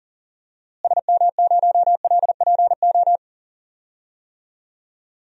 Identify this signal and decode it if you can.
Morse code, heard as a keyed tone, about 39 wpm, 700 Hz. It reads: SM0LPO
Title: DXCC-100 Call Signs - 6 Characters - Repeated 2x - 40wpm